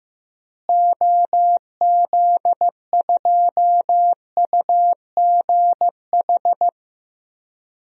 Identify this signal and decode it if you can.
Morse code, heard as a keyed tone, about 15 wpm, 700 Hz. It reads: OZ2UGH
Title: DXCC-100 Call Signs - 6 Characters - Repeated 3x - 15wpm